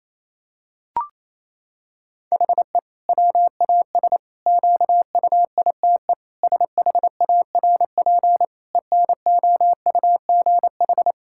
Morse code, heard as a keyed tone, about 27 wpm, 700 Hz. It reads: HE WAS QUITE SHARP ENOUGH
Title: Sentences from Top 1600 Words 28wpm